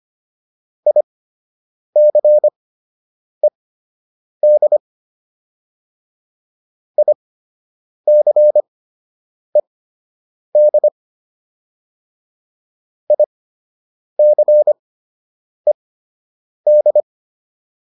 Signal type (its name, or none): Morse code